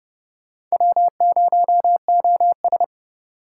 Morse code, heard as a keyed tone, about 39 wpm, 700 Hz. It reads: W0OS